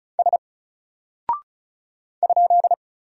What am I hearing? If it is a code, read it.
Morse code, heard as a keyed tone, about 36 wpm, 700 Hz. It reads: S ?
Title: Single Character (Letter:Number:Punctuation:Prosign) Alt 35wpm